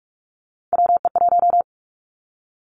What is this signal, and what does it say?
Morse code, heard as a keyed tone, about 46 wpm, 700 Hz. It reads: WE1
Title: Three Challenging Letter-Number Combinations 45wpm